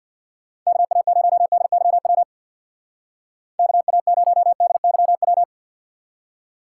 Morse code, heard as a keyed tone, about 64 wpm, 700 Hz. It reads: KA0DYW KA0DYW